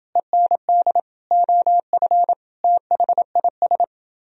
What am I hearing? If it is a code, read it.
Morse code, heard as a keyed tone, about 29 wpm, 700 Hz. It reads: END OF THIS